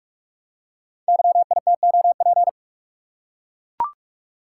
Morse code, heard as a keyed tone, about 44 wpm, 690 Hz. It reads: YITOP
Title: Code Groups - 5 Letters - 45wpm